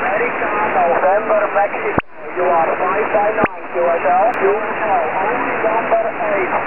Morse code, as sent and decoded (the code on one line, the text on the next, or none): none